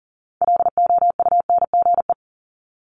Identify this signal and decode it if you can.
Morse code, heard as a keyed tone, about 38 wpm, 700 Hz. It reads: LOUNGE
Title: Multi-letter Phonemes Repeated 3x 40wpm